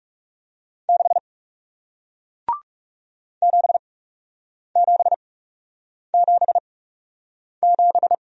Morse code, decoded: 6 7777

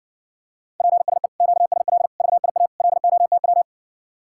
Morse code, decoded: PSE QSL VIA LOTW

